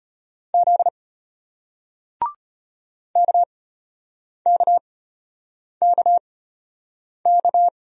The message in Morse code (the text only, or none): Z KKKK